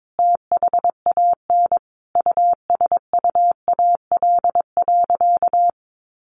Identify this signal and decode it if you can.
Morse code, heard as a keyed tone, about 22 wpm, 700 Hz. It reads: THAN USUAL.
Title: Sentences from Top 1400 Words 22wpm